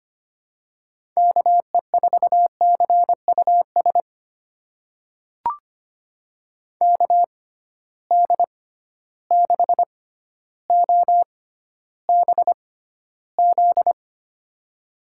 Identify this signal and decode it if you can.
Morse code, heard as a keyed tone, about 25 wpm, spaced as for 13 wpm, 700 Hz. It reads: KE4CUS KD6OBZ
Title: USA Call Signs - 6 Characters - 6x Character Spacing - 25wpm